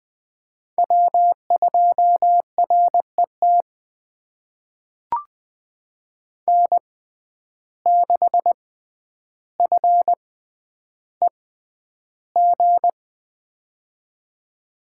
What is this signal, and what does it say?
Morse code, heard as a keyed tone, about 20 wpm, spaced as for 10 wpm, 700 Hz. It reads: W2RET N6FEG
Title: USA Call Signs - 5 Characters - 6x Character Spacing - 20wpm